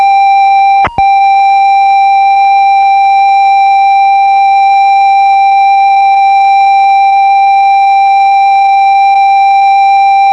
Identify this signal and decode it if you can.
Computer data on cassette.